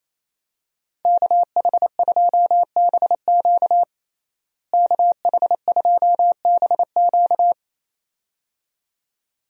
Morse code, heard as a keyed tone, about 28 wpm, 700 Hz. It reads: KH2BQ KH2BQ
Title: USA Call Signs - 5 Characters - Repeated 2x - 28wpm